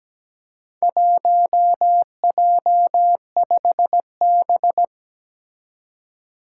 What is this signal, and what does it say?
Morse code, heard as a keyed tone, about 17 wpm, 700 Hz. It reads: 1J5B
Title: Four Challenging Letter-Number Combinations 17wpm